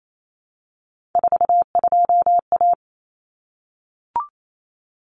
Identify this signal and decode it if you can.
Morse code, heard as a keyed tone, about 28 wpm, 700 Hz. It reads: 42A
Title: Common Cut Numbers - 3-Digit 28wpm